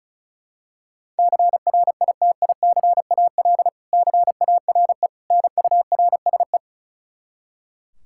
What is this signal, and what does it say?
Morse code, heard as a keyed tone, about 34 wpm, 700 Hz. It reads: CRITICAL CARE NURSE
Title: K1USN SST QSO Practice 35wpm